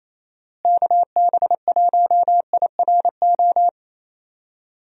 Morse code, heard as a keyed tone, about 28 wpm, 700 Hz. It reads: KB1IRO
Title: USA Call Signs - 6 Characters - Repeated 3x - 28wpm